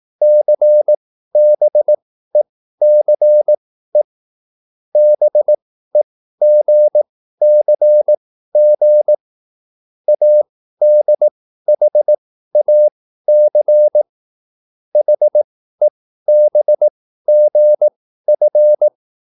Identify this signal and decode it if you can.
Morse code, heard as a keyed tone, about 18 wpm, spaced as for 13 wpm, 600 Hz. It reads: CBECE BEGCG ADHAC HEBGF
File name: QSL / CW lesson